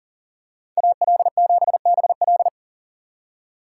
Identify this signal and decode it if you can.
Morse code, heard as a keyed tone, about 40 wpm, 700 Hz. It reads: AL7BL
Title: USA Call Signs - 5 Characters - 3x Character Spacing - 40wpm